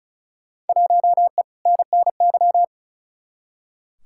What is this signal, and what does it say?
Morse code, heard as a keyed tone, about 34 wpm, 700 Hz. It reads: 1E NNY
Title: Field Day Exchanges - 35wpm